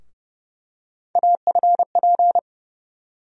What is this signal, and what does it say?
Morse code, heard as a keyed tone, about 30 wpm, 700 Hz. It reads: AFP
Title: USA Call Sign Postfixes - 3 Letters 30wpm